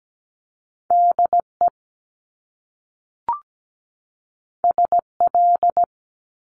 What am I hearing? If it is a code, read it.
Morse code, heard as a keyed tone, about 17 wpm, 700 Hz. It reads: DE SL
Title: Top Two Letter Combinations in Words 17wpm